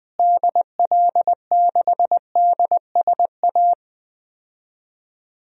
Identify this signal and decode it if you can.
Morse code, heard as a keyed tone, about 20 wpm, 700 Hz. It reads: DL6DSA
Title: DXCC-100 Call Signs - 6 Characters - Repeated 2x - 20wpm